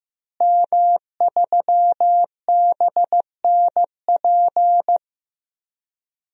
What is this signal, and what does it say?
Morse code, heard as a keyed tone, about 15 wpm, 700 Hz. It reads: M3BNP
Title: DXCC-100 Call Signs - 5 Characters - Repeated 3x - 15wpm